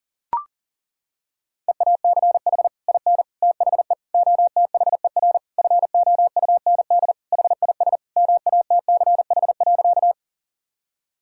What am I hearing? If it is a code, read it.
Morse code, heard as a keyed tone, about 37 wpm, 700 Hz. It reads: EACH IN THE OTHER FOUND HIS MATCH.
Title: Sentences from Top 1600 Words 40wpm